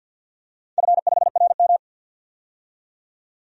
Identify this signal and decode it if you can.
Morse code, heard as a keyed tone, about 48 wpm, 700 Hz. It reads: F5RM
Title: DXCC-100 Call Signs - 4 Characters - 50wpm